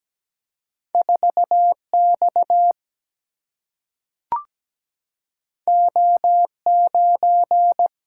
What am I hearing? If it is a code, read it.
Morse code, heard as a keyed tone, about 17 wpm, 700 Hz. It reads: 4 X O 9
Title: Single-Letter Single-Number 17wpm